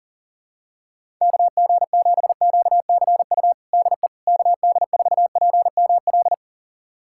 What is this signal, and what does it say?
Morse code, heard as a keyed tone, about 40 wpm, 700 Hz. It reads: KG7QCU DE KD4PML
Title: USA Callsign DE Callsign - 6 Characters - 40wpm